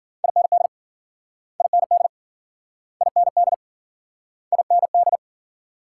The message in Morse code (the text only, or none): IND IND IND IND